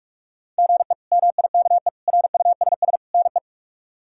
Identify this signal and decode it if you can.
Morse code, heard as a keyed tone, about 43 wpm, 700 Hz. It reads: GE MIKE RUSS NE
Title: K1USN SST QSO Practice - 45wpm